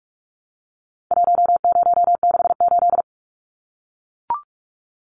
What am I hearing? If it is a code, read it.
Morse code, heard as a keyed tone, about 44 wpm, 700 Hz. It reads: 1068
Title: Numbers - 4-Digit 45wpm